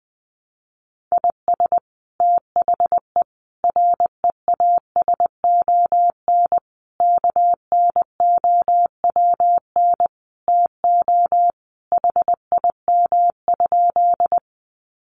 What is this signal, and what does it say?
Morse code, heard as a keyed tone, about 20 wpm, 700 Hz. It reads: IS THE REASON KNOWN TO HIM?
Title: Sentences from Top 700 Words 20wpm